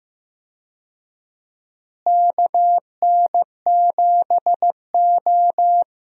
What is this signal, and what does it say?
Morse code, heard as a keyed tone, about 15 wpm, 700 Hz. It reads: KN7O
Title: USA Call Signs - 4 Characters - Repeated 2x - 15wpm